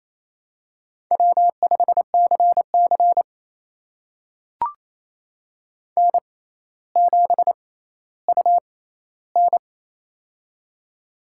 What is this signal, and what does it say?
Morse code, heard as a keyed tone, about 27 wpm, spaced as for 14 wpm, 700 Hz. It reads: W5CC N7UN
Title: USA Call Signs - 4 Characters - 6x Character Spacing - 28wpm